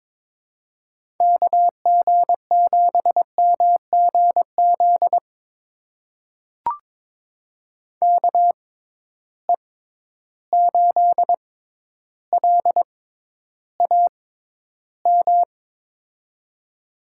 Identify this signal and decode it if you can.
Morse code, heard as a keyed tone, about 22 wpm, spaced as for 11 wpm, 700 Hz. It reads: KG7MGZ KE8LAM
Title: USA Call Signs - 6 Characters - 6x Character Spacing - 22wpm